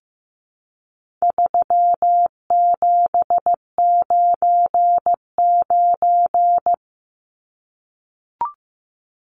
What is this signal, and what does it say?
Morse code, heard as a keyed tone, about 15 wpm, 700 Hz. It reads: 3799
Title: Numbers - 4-Digit 15wpm